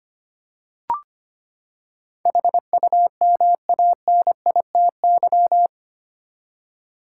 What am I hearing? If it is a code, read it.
Morse code, heard as a keyed tone, about 25 wpm, 700 Hz. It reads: HUMANITY
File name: Eight Letter Words - 25wpm